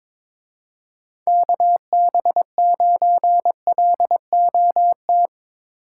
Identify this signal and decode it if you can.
Morse code, heard as a keyed tone, about 24 wpm, 700 Hz. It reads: KB9LOT
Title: Speed Racing - USA Call Signs 22wpm